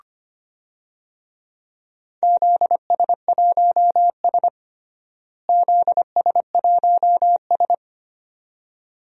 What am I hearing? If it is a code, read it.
Morse code, heard as a keyed tone, about 25 wpm, 700 Hz. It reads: ZS1S ZS1S
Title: DXCC-100 Call Signs - 4 Characters - Repeated 2x - 25wpm